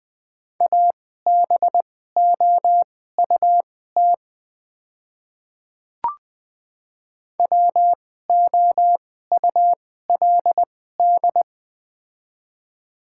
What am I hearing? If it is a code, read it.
Morse code, heard as a keyed tone, about 21 wpm, spaced as for 15 wpm, 690 Hz. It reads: ABOUT WOULD
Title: 2x Character Spacing - Top 200 Words - 20wpm